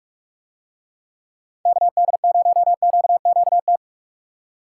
Morse code, heard as a keyed tone, about 45 wpm, 700 Hz. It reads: KD0QQT